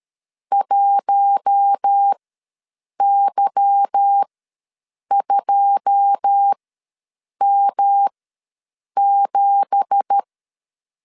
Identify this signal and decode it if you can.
Morse code, heard as a keyed tone, about 13 wpm, spaced as for 7 wpm, 800 Hz. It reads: 1Y2M7
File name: LEZIONI / 9 2867 ws 12wpm 3431